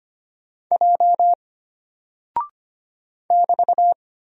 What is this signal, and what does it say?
Morse code, heard as a keyed tone, about 25 wpm, 700 Hz. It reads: J <BT>
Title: Single Character (Letter:Number:Punctuation:Prosign) Alt 25wpm